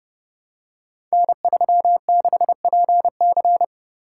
Morse code, heard as a keyed tone, about 30 wpm, 700 Hz. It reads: N36PC
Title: Code Groups - 5 Letter:Numbers - 30wpm